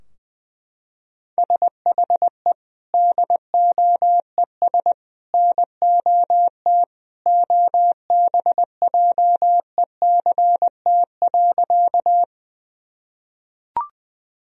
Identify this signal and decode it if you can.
Morse code, heard as a keyed tone, about 20 wpm, 700 Hz. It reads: SHE DOES NOT OBJECT.
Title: Sentences from Top 1000 Words - Review Part 3 - 20wpm